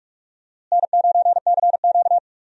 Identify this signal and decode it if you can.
Morse code, heard as a keyed tone, about 45 wpm, 700 Hz. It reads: N9CQ
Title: USA Call Signs - 4 Characters - Repeated 2x - 45wpm